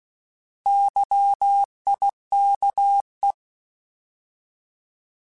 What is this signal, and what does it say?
Morse code, heard as a keyed tone, about 16 wpm, 800 Hz. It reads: YIKE
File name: Alphabetical / Alphabetical mp3s / Ys @15